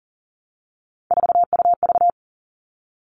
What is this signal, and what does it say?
Morse code, heard as a keyed tone, about 40 wpm, 700 Hz. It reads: 4UV
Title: All Cut Numbers - 3-Digit 40wpm